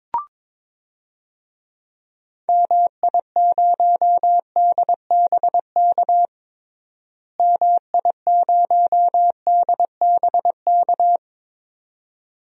Morse code, heard as a keyed tone, about 21 wpm, 700 Hz. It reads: MI0DBK MI0DBK
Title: DXCC-100 Call Signs - 6 Characters - Repeated 2x - 22wpm